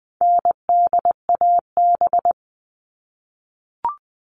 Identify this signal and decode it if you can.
Morse code, heard as a keyed tone, about 20 wpm, 700 Hz. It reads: NDAB
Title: All Cut Numbers - 4-Digit 20wpm